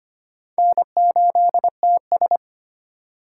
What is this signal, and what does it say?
Morse code, heard as a keyed tone, about 31 wpm, 700 Hz. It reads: N8TS
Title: Speed Racing - USA 1x2 Call Signs 25wpm